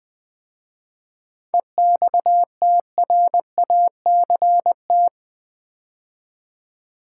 Morse code, heard as a keyed tone, about 20 wpm, 700 Hz. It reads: EXTRACT
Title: 7 Letter Words - First 3 Letters Given - 20wpm